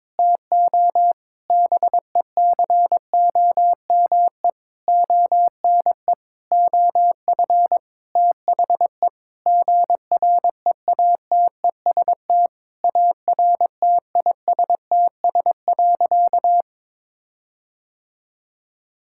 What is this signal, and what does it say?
Morse code, heard as a keyed tone, about 22 wpm, 700 Hz. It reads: TO BECOME ONE OF THE GREATEST ARTISTS.
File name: practice / 22wpm / Sentences with 8-Word Phrases - 22wpm